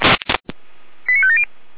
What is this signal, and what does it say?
Five-tone selective call (ZVEI), heard as quick runs of five tones, tones of about 70 ms. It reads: 89570
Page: /455/SDR/ZVEI/88192/2022.02.21/